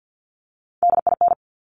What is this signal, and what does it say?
Morse code, heard as a keyed tone, about 50 wpm, 700 Hz. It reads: DIN
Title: Top Three Letter Combinations in Words 50wpm